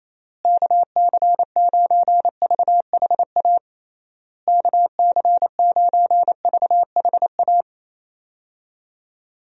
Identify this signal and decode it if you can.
Morse code, heard as a keyed tone, about 27 wpm, 700 Hz. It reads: KC9VHA KC9VHA
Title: USA Call Signs - 6 Characters - Repeated 3x - 28wpm